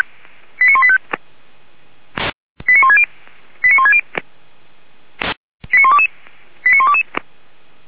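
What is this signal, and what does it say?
Five-tone selective call (ZVEI), heard as quick runs of five tones, tones of about 70 ms. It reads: 89185 89160 89160 89122 89122
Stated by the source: ZVEI five-tone selective call 89136 received on SDR, from the audio archive of feuerwehr-orlamuende (fire brigade Orlamünde).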